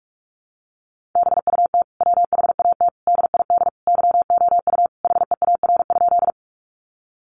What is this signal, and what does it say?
Morse code, heard as a keyed tone, about 44 wpm, 700 Hz. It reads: BUT WHAT DID YOU HEAR?
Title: Sentences from Top 300 Words 45wpm